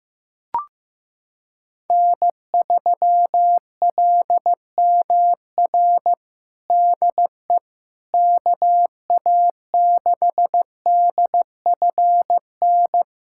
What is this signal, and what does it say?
Morse code, heard as a keyed tone, about 15 wpm, 700 Hz. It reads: N3LMR DE KA6DFN
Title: USA Callsign DE Callsign 15wpm